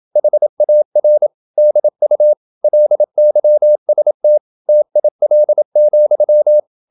Morse code, heard as a keyed tone, about 27 wpm, 600 Hz. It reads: HAR DU LYST TIL,
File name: Ferie QRQ 27wpm